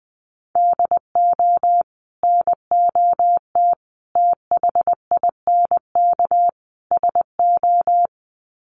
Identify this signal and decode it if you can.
Morse code, heard as a keyed tone, about 20 wpm, 700 Hz. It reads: DO NOT THINK SO
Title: Sets of 4 Words 20wpm